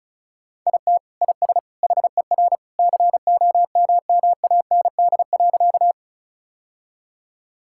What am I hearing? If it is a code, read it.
Morse code, heard as a keyed tone, about 34 wpm, 700 Hz. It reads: IT IS HER COMMAND.
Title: Sentences from Top 1600 Words 35wpm